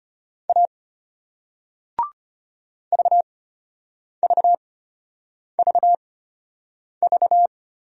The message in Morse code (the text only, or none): A VVVV